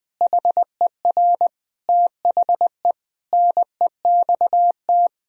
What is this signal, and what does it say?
Morse code, heard as a keyed tone, about 21 wpm, 700 Hz. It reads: HER THE NEXT
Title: Sentences from Top 1000 Words - Review Part 1 - 20wpm